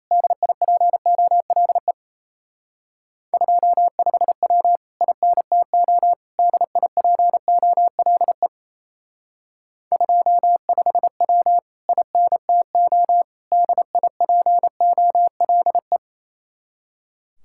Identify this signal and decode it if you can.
Morse code, heard as a keyed tone, about 35 wpm, 700 Hz. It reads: DIPOLE 25W INTO DIPOLE 25W INTO DIPOLE
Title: Speed-Racing Common QSO Phrases - 28wpm